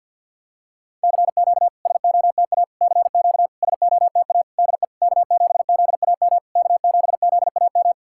Morse code, heard as a keyed tone, about 49 wpm, 700 Hz. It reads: CQ SOTA CQ SOTA DE K7ZAM K7ZAM
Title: SOTA QSO Practice 50wpm